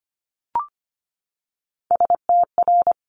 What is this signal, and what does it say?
Morse code, heard as a keyed tone, about 25 wpm, 700 Hz. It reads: STR
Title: Multi-letter Phonemes 25wpm